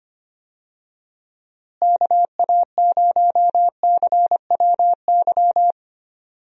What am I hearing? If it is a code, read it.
Morse code, heard as a keyed tone, about 25 wpm, 700 Hz. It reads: KA0CWY
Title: USA Call Signs - 6 Characters - Repeated 3x - 25wpm